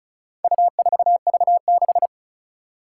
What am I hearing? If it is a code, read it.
Morse code, heard as a keyed tone, about 36 wpm, 700 Hz. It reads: U4V6
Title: Four Challenging Letter-Number Combinations 35wpm